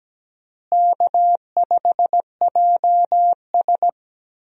Morse code, heard as a keyed tone, about 17 wpm, 700 Hz. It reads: K5JS
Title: USA Call Signs - 4 Characters - Repeated 3x - 17wpm